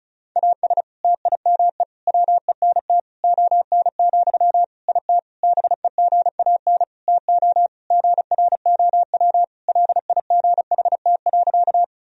Morse code, heard as a keyed tone, about 36 wpm, 700 Hz. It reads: AS TIME WENT ON, IT BEGAN TO GROW LIGHT.
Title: Sentences from Top 1000 Words - Review Part 1 - 35wpm